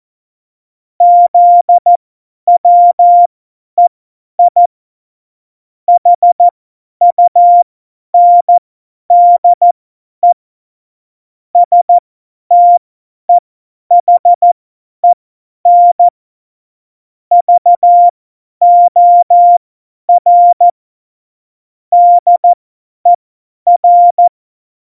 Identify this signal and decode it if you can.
Morse code, heard as a keyed tone, about 14 wpm, spaced as for 10 wpm, 700 Hz. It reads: ZWEI HUNDE STEHEN VOR DER
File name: Morsekurs / Witze / Witz 09. Oktober 2018, 50 F70